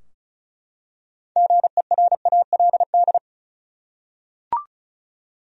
Morse code, heard as a keyed tone, about 36 wpm, 700 Hz. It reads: GERALD
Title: USA Common Names - 35wpm